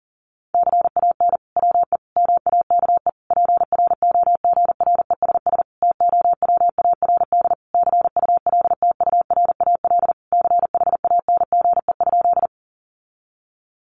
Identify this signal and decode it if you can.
Morse code, heard as a keyed tone, about 40 wpm, 700 Hz. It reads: CAN WE MAKE PROGRESS TOWARD CULTURAL CHANGE?
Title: Sentences from Top 1500 Words 40wpm